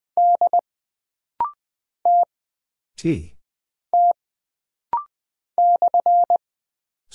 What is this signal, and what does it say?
Morse code, heard as a keyed tone, about 20 wpm, 700 Hz. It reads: D T T /
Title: Single Character (Letter:Number:Punctuation:Prosign) - Rapid-Fire - 20wpm